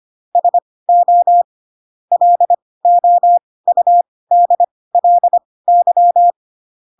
Morse code, heard as a keyed tone, about 25 wpm, spaced as for 18 wpm, 700 Hz. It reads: SO LOUDLY
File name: Pinocchio / Pinocchio ch7 12 25 ik1hge